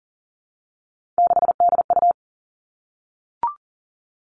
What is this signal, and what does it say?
Morse code, heard as a keyed tone, about 40 wpm, 700 Hz. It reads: 6DU
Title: All Cut Numbers - 3-Digit 40wpm